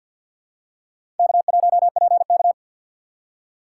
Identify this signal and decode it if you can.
Morse code, heard as a keyed tone, about 53 wpm, 700 Hz. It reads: K1PK
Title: USA Call Signs - 4 Characters - 6x Character Spacing - 50wpm